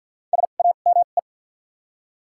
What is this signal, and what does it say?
Morse code, heard as a keyed tone, about 48 wpm, spaced as for 35 wpm, 700 Hz. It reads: SAME
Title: 2x Character Spacing - Top 300 Words - 50wpm